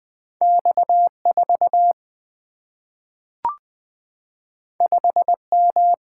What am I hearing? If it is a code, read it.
Morse code, heard as a keyed tone, about 20 wpm, 700 Hz. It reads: X4 5M
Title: Single-Letter Single-Number 20wpm